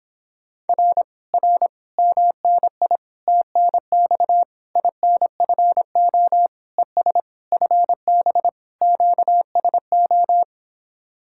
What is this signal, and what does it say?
Morse code, heard as a keyed tone, about 26 wpm, 700 Hz. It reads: R R MNI TNX INFO ES FB QSO